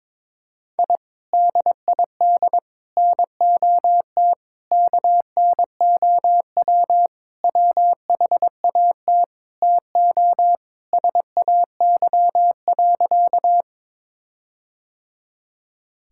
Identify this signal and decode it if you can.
Morse code, heard as a keyed tone, about 22 wpm, 700 Hz. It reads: I DID NOT KNOW WHAT TO SAY.
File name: practice / 22wpm / Sentences with 4-Word Phrases - 22wpm